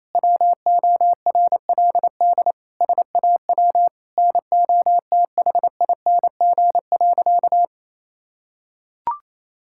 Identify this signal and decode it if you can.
Morse code, heard as a keyed tone, about 27 wpm, 700 Hz. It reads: WORLD SAW NOTHING.
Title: Sentences from Top 1000 Words - Review Part 6 - 28wpm